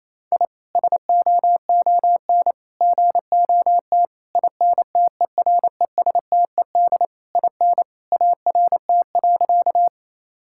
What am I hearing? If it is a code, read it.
Morse code, heard as a keyed tone, about 28 wpm, 700 Hz. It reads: I SOON GOT INTERESTED IN ART.
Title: Sentences from Top 1000 Words - Review Part 6 - 28wpm